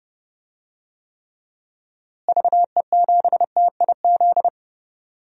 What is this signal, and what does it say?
Morse code, heard as a keyed tone, about 30 wpm, 700 Hz. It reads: VE7TIZ
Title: DXCC-100 Call Signs - 6 Characters - Repeated 2x - 30wpm